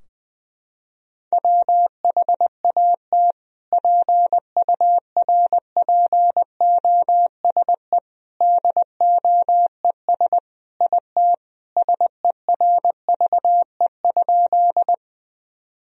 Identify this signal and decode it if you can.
Morse code, heard as a keyed tone, about 20 wpm, 700 Hz. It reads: WHAT PURPOSE DOES IT SERVE?
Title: Sentences from Top 1000 Words - Review Part 2 - 20wpm